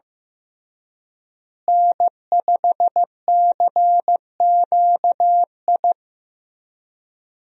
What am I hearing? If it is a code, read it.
Morse code, heard as a keyed tone, about 15 wpm, 700 Hz. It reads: N5CQI
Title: USA Call Signs - 5 Characters - Repeated 3x - 15wpm